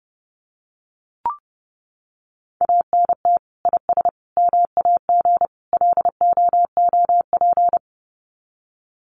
Morse code, heard as a keyed tone, about 30 wpm, 700 Hz. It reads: ANT IS MAG LOOP
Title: Common QSO Phrases 30wpm